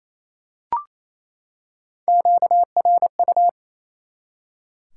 Morse code, heard as a keyed tone, about 28 wpm, 700 Hz. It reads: QRU
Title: Top 90 QSO Elements 28wpm